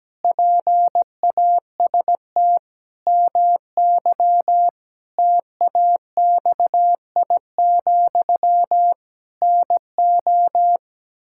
Morse code, heard as a keyed tone, about 17 wpm, 700 Hz. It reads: PAST MY TAXI, NO
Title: Pangram Challenge v2 - 17wpm